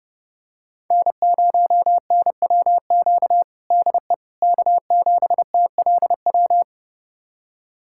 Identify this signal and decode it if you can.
Morse code, heard as a keyed tone, about 30 wpm, 700 Hz. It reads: N0NWQ DE K7TLW
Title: USA Callsign DE Callsign - 5 Characters - 30wpm